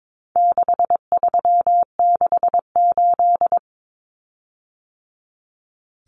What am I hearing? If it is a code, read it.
Morse code, heard as a keyed tone, about 22 wpm, 700 Hz. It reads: 6368
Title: Common Cut Numbers - 4-Digit 22wpm